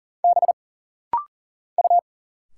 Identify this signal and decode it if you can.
Morse code, heard as a keyed tone, about 40 wpm, 700 Hz. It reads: B U
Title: Single Letters - Rapid-Fire 40wpm